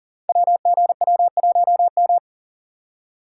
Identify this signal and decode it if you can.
Morse code, heard as a keyed tone, about 41 wpm, 690 Hz. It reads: WGW1M
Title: Code Groups - 5 Letter:Numbers - 40wpm